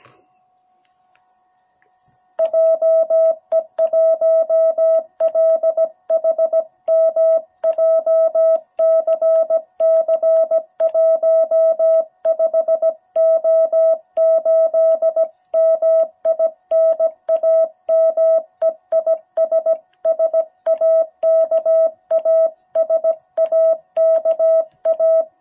Morse code, heard as a keyed tone, about 17 wpm, 660 Hz. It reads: JE1LHM JCC15O8 MI NAME IS SAKA SAKA
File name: jh8cbh / 2021cw